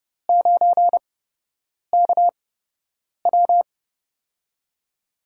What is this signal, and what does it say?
Morse code, heard as a keyed tone, about 30 wpm, spaced as for 8 wpm, 700 Hz. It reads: 9KW
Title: USA Call Signs - 5 Characters - 8x Character Spacing - 30wpm